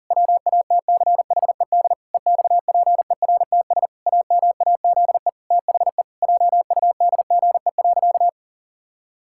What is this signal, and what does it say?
Morse code, heard as a keyed tone, about 40 wpm, 700 Hz. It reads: WATCHED EXPERTS AMAZE THE JUDGE.
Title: Pangram Challenge v2 - 40wpm